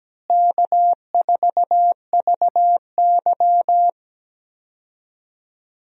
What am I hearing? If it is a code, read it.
Morse code, heard as a keyed tone, about 17 wpm, 700 Hz. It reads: K4VY
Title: USA Call Signs - 4 Characters - Repeated 2x - 17wpm